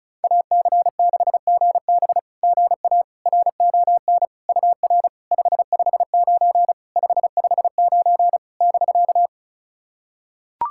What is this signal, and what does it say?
Morse code, heard as a keyed tone, about 36 wpm, 700 Hz. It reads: AC6GB GA RON UR 559 559 <BK>
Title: SOTA QSO Practice 35wpm